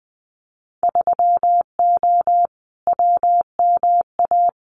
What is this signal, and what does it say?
Morse code, heard as a keyed tone, about 21 wpm, 700 Hz. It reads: 3O WMA
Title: Winter Field Day Exchanges Encore 20wpm